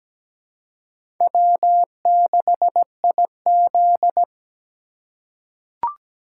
Morse code, heard as a keyed tone, about 17 wpm, 700 Hz. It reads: W6IZ
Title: USA Call Signs - 4 Characters - 4x Character Spacing - 17wpm